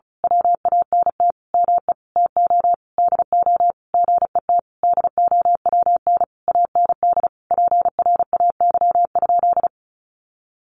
Morse code, heard as a keyed tone, about 34 wpm, 700 Hz. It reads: WANT ME TO DO GET DOWN AND PRAY?
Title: A Sound Of Thunder 35wpm